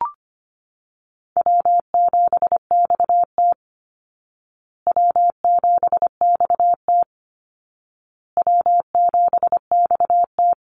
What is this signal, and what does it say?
Morse code, heard as a keyed tone, about 25 wpm, 700 Hz. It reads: W7XT W7XT W7XT
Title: Easy Call Signs Repeated 3x - 25wpm